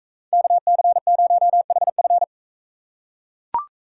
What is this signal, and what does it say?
Morse code, heard as a keyed tone, about 44 wpm, 700 Hz. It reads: KC0HF